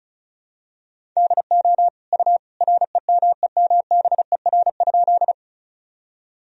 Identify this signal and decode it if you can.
Morse code, heard as a keyed tone, about 36 wpm, 700 Hz. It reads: DO U REMEMBER?
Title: Sentences from Top 400 Words and Abbreviations 35wpm